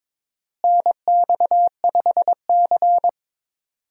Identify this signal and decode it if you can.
Morse code, heard as a keyed tone, about 22 wpm, 700 Hz. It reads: NX5C
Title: USA Call Signs - 4 Characters - Repeated 2x - 22wpm